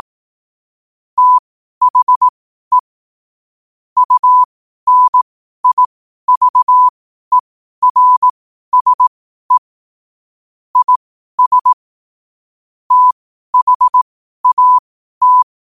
Morse code, heard as a keyed tone, about 18 wpm, spaced as for 12 wpm, 1000 Hz. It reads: THE UNIVERSE IS THAT
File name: qotdfiles / 12wpm